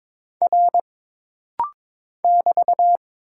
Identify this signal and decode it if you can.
Morse code, heard as a keyed tone, about 22 wpm, 700 Hz. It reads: R <BT>
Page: This rapid-fire timing is geared to push your ICR (Instant Character Recognition) skills.